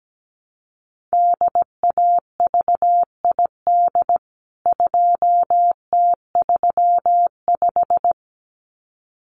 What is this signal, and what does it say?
Morse code, heard as a keyed tone, about 17 wpm, 700 Hz. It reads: DAVID 2T35
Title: CWT Exchange 17wpm